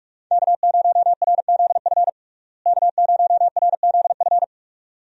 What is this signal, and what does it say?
Morse code, heard as a keyed tone, about 46 wpm, 700 Hz. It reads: K0RZF K0RZF
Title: USA Call Signs - 5 Characters - Repeated 2x - 45wpm